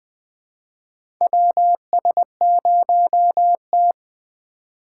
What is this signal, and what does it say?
Morse code, heard as a keyed tone, about 20 wpm, 700 Hz. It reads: WS0T
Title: USA Call Signs - 2x Character Spacing - 20wpm